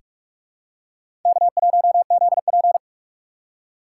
Morse code, heard as a keyed tone, about 41 wpm, 700 Hz. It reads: K1ZP
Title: USA Call Signs - 4 Characters - 4x Character Spacing - 45wpm